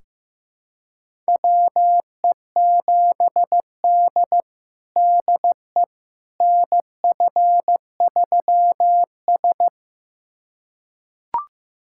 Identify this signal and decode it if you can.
Morse code, heard as a keyed tone, about 15 wpm, 700 Hz. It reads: WE7D DE NF3S
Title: USA Callsign DE Callsign - 4 Characters - 15wpm